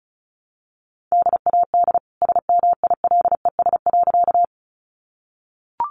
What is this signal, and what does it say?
Morse code, heard as a keyed tone, about 34 wpm, 700 Hz. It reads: DAD SMILES.
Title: Easy Phrases 35wpm